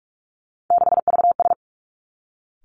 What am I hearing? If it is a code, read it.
Morse code, heard as a keyed tone, about 46 wpm, 700 Hz. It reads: 6VS
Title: Three Challenging Letter-Number Combinations 45wpm